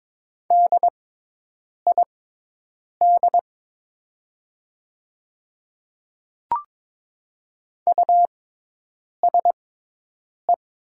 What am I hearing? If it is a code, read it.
Morse code, heard as a keyed tone, about 22 wpm, spaced as for 8 wpm, 700 Hz. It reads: DID USE
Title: 6x Character Spacing - Top 100 Words - 22wpm